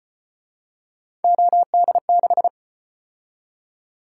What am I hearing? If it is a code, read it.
Morse code, heard as a keyed tone, about 33 wpm, 700 Hz. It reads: OD6